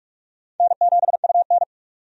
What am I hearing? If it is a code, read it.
Morse code, heard as a keyed tone, about 46 wpm, 700 Hz. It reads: N7UN
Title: USA Call Signs - 4 Characters - 45wpm